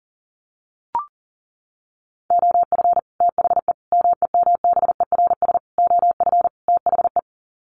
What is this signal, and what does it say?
Morse code, heard as a keyed tone, about 40 wpm, 700 Hz. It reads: OF THE MEMBERS OF THE
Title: Sets of 5 Words Encore2 40wpm